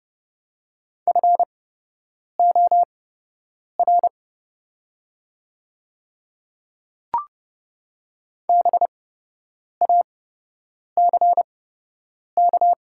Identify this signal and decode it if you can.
Morse code, heard as a keyed tone, about 30 wpm, spaced as for 8 wpm, 700 Hz. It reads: FOR BACK
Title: 8x Character Spacing - Top 200 Words - 30wpm